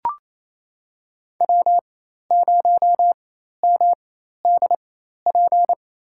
Morse code, heard as a keyed tone, about 28 wpm, spaced as for 13 wpm, 700 Hz. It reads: W0MDP
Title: USA Call Signs - 4x Character Spacing - 28wpm